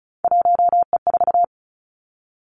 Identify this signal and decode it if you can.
Morse code, heard as a keyed tone, about 36 wpm, 700 Hz. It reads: 1E4
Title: Three Challenging Letter-Number Combinations 35wpm